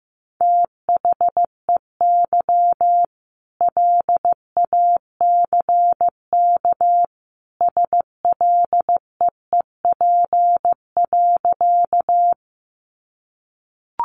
Morse code, heard as a keyed tone, about 15 wpm, 700 Hz. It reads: THEY LACK SLEEP.
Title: Sentences from Top 1100 Words 15wpm